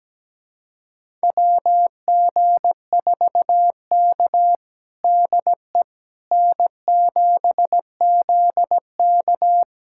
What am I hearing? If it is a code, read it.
Morse code, heard as a keyed tone, about 17 wpm, 700 Hz. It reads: WG4K DE N7ZK
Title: USA Callsign DE Callsign - 4 Characters - 17wpm